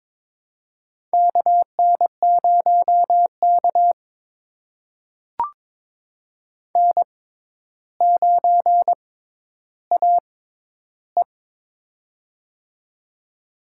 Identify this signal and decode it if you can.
Morse code, heard as a keyed tone, about 22 wpm, spaced as for 11 wpm, 700 Hz. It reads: KN0K N9AE
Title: USA Call Signs - 4 Characters - 6x Character Spacing - 22wpm